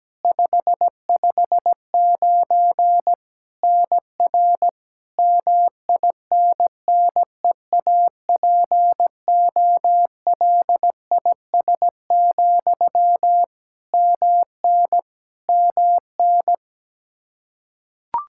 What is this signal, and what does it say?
Morse code, heard as a keyed tone, about 17 wpm, 700 Hz. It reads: 559 NR MINNEAPOLIS, MN MN
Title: Practice QSO - Exchange 1 - Beginning - 17wpm